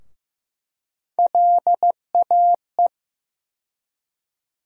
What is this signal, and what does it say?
Morse code, heard as a keyed tone, about 15 wpm, 700 Hz. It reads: LAE
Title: USA Call Sign Postfixes - 3 Letters 15wpm